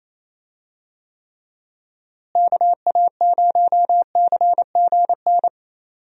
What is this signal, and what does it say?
Morse code, heard as a keyed tone, about 28 wpm, 700 Hz. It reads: KA0CGN
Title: USA Call Signs - 6 Characters - Repeated 3x - 28wpm